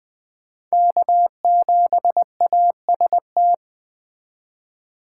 Morse code, heard as a keyed tone, about 20 wpm, 700 Hz. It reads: K7AST
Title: Speed Racing - USA 1x3 Call Signs 17wpm